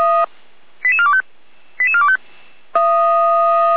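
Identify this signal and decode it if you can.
Five-tone selective call (ZVEI), heard as quick runs of five tones, tones of about 70 ms. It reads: 88425 88425
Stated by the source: ZVEI five-tone call 88405, received on SDR.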